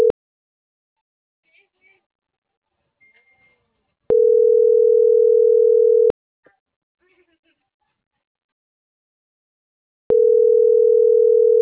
Ringback tone (North America), 440 + 480 Hz.